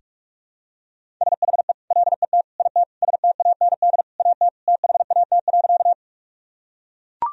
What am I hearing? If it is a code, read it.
Morse code, heard as a keyed tone, about 42 wpm, 700 Hz. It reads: SHE LET IT STAND AT THAT.
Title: Sentences from Top 1000 Words - Review Part 1 - 45wpm